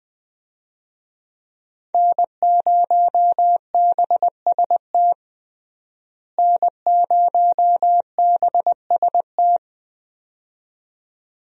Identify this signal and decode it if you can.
Morse code, heard as a keyed tone, about 20 wpm, 700 Hz. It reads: N0BST N0BST
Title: USA Call Signs - 5 Characters - Repeated 2x - 20wpm